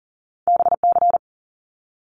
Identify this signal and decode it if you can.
Morse code, heard as a keyed tone, about 42 wpm, 700 Hz. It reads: BC
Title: Canadian Provinces and Territories 40wpm